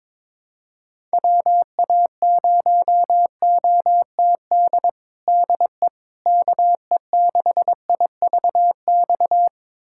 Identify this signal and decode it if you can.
Morse code, heard as a keyed tone, about 22 wpm, 700 Hz. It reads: WA0OTD DE KE6IVX
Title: USA Callsign DE Callsign 22wpm